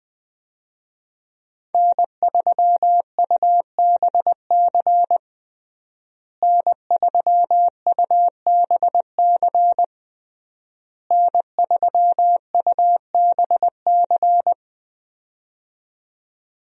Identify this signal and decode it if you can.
Morse code, heard as a keyed tone, about 21 wpm, 700 Hz. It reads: N3UBC N3UBC N3UBC
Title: USA Call Signs - 5 Characters - Repeated 3x - 20wpm